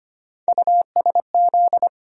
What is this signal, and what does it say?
Morse code, heard as a keyed tone, about 25 wpm, 700 Hz. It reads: USZ
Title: USA Call Sign Postfixes - 3 Letters 25wpm